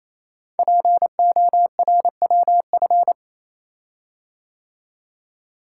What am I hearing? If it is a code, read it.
Morse code, heard as a keyed tone, about 27 wpm, 700 Hz. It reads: PORWF
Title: Code Groups - 5 Letters - 28wpm